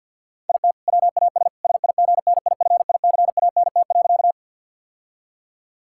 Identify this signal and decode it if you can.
Morse code, heard as a keyed tone, about 50 wpm, 700 Hz. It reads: IT WAS SIGNIFICANT.
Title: Sentences from Top 1000 Words - Review Part 6 - 50wpm